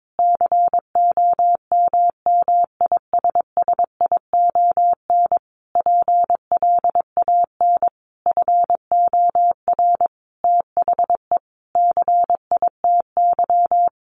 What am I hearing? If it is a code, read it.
Morse code, heard as a keyed tone, about 22 wpm, 700 Hz. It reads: COMMISSION PLAN FOR THE CITY
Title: Sentences from Top 1400 Words 22wpm